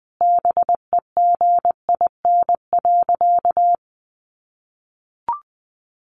Morse code, heard as a keyed tone, about 20 wpm, 700 Hz. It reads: BEGIN.
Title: Sentences from Top 600 Words 20wpm